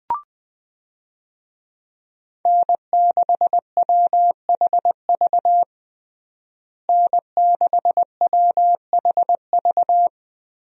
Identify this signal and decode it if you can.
Morse code, heard as a keyed tone, about 21 wpm, 700 Hz. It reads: N6WHV N6WHV
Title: USA Call Signs - 5 Characters - Repeated 2x - 20wpm